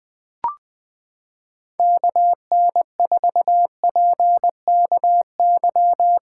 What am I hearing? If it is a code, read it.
Morse code, heard as a keyed tone, about 20 wpm, 700 Hz. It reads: KN4PKY
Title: USA Callsign DE Callsign - 6 Characters - 20wpm